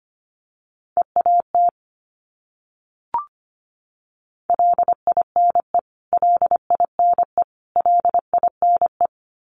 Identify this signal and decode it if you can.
Morse code, heard as a keyed tone, about 25 wpm, 700 Hz. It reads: EAT LINE LINE LINE
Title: Top 500 Words Repeated 25wpm